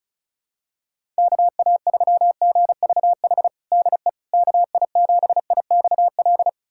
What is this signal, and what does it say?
Morse code, heard as a keyed tone, about 34 wpm, 700 Hz. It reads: KA3GVH DE KI7IXL
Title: USA Callsign DE Callsign - 6 Characters - 35wpm